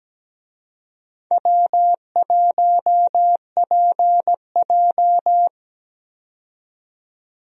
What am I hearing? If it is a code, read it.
Morse code, heard as a keyed tone, about 17 wpm, 700 Hz. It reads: W1PJ
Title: USA Call Signs - 4 Characters - 17wpm